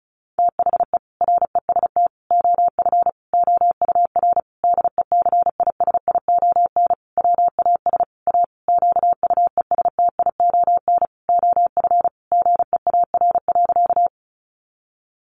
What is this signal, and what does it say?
Morse code, heard as a keyed tone, about 34 wpm, 700 Hz. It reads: THE REST OF OUR DECISION WAS A QUESTION OF GEAR.
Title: Sentences from Top 1400 Words 35wpm